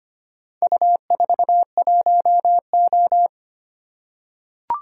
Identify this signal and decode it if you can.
Morse code, heard as a keyed tone, about 25 wpm, 700 Hz. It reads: U41O
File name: practice / 25wpm / Four Letter-Number Combinations - 25wpm